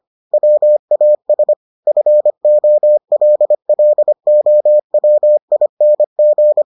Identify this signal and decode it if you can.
Morse code, heard as a keyed tone, about 25 wpm, 600 Hz. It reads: WAS FOLLOWING